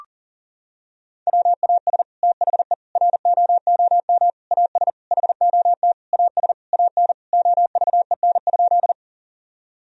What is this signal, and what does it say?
Morse code, heard as a keyed tone, about 40 wpm, 700 Hz. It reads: WAS THE ROOM AS HOT AS AN OVEN?
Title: Sentences from Top 1000 Words - Review Part 4 - 40wpm